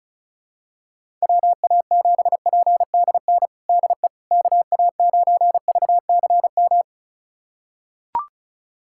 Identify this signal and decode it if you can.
Morse code, heard as a keyed tone, about 36 wpm, 700 Hz. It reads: WA7PDN DE KA9VCM
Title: USA Callsign DE Callsign 35wpm